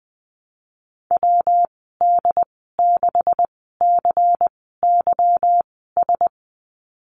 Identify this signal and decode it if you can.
Morse code, heard as a keyed tone, about 20 wpm, spaced as for 15 wpm, 700 Hz. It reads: WD6CYS
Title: 167 Lesson 27g - Number 7 - Call Signs 20wpm